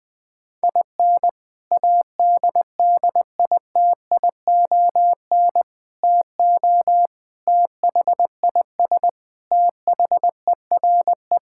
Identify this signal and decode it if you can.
Morse code, heard as a keyed tone, about 21 wpm, 700 Hz. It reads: IN ADDITION TO THIS THERE
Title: Sentences with 3-Word Phrases - 20wpm